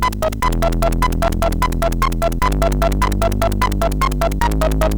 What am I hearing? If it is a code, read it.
There is no code.